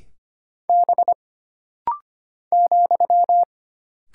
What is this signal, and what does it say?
Morse code, heard as a keyed tone, about 25 wpm, 700 Hz. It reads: B ,
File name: Single Character (Letter:Number:Punctuation:Prosign) - Rapid-Fire - 25wpm